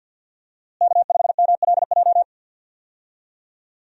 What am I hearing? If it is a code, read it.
Morse code, heard as a keyed tone, about 47 wpm, 700 Hz. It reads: K5MLJ
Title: USA Call Signs - 5 Characters - Repeated 2x - 50wpm